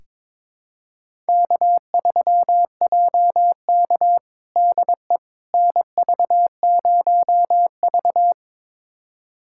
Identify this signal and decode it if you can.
Morse code, heard as a keyed tone, about 21 wpm, 700 Hz. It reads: K3JK DE NV0V
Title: USA Callsign DE Callsign - 4 Characters - 22wpm